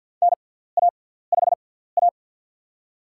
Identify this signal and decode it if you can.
Morse code, heard as a keyed tone, about 52 wpm, spaced as for 17 wpm, 700 Hz. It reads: NA5A
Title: USA Call Signs - 4 Characters - 6x Character Spacing - 50wpm